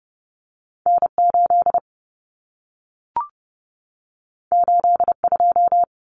Morse code, heard as a keyed tone, about 30 wpm, 700 Hz. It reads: N8 82
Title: Common Cut Numbers - 2-Digit 30wpm